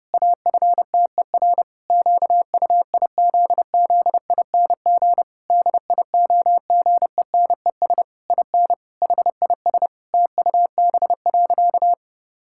Morse code, heard as a keyed tone, about 30 wpm, 700 Hz. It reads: AFTER QUIZZING DIOGENES IN HIS TUB.
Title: Pangram Challenge v2 - 30wpm